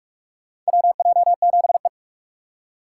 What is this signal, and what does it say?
Morse code, heard as a keyed tone, about 46 wpm, 700 Hz. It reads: WJ7E
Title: USA Call Signs - 4 Characters - 4x Character Spacing - 45wpm